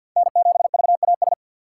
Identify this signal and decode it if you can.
Morse code, heard as a keyed tone, about 52 wpm, 700 Hz. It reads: N7VAS
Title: USA Call Signs - 5 Characters - 3x Character Spacing - 50wpm